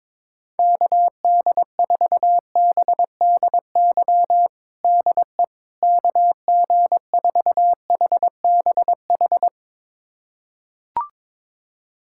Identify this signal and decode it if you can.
Morse code, heard as a keyed tone, about 22 wpm, 700 Hz. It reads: KD4BDY DE KG4HBH
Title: USA Callsign DE Callsign 22wpm